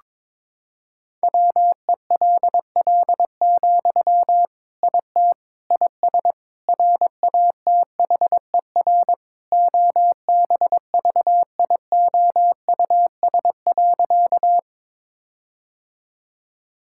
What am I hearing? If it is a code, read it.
Morse code, heard as a keyed tone, about 21 wpm, 700 Hz. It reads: WELL, IT IS RATHER OBVIOUS.
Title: Sentences from Top 1600 Words 22wpm